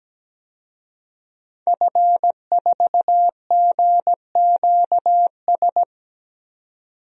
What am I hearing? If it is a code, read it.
Morse code, heard as a keyed tone, about 17 wpm, 700 Hz. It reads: F4GQS
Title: DXCC-100 Call Signs - 5 Characters - 17wpm